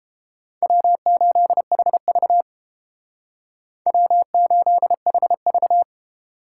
Morse code, heard as a keyed tone, about 29 wpm, 700 Hz. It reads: W8HV W8HV